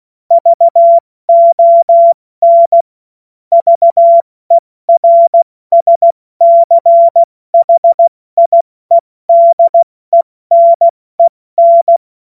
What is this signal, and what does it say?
Morse code, heard as a keyed tone, about 16 wpm, 680 Hz. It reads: VON VERSCHIEDENEN
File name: Morsekurs / Witze / Witz 07. Juni 2022 70 F80